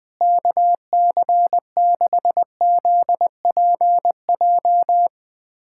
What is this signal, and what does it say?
Morse code, heard as a keyed tone, about 20 wpm, 700 Hz. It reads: KC6ZPJ
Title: USA Callsign DE Callsign - 6 Characters - 20wpm